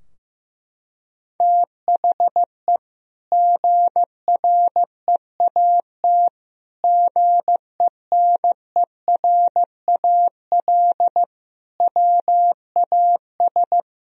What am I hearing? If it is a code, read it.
Morse code, heard as a keyed tone, about 16 wpm, 700 Hz. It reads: THE GREAT GENERAL WAS
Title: Sentences from Top 1000 Words - Review Part 4 - 15wpm